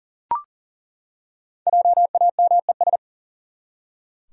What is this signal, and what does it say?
Morse code, heard as a keyed tone, about 40 wpm, 700 Hz. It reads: JAMES
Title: USA Common Names - Max 5 Letters - 40wpm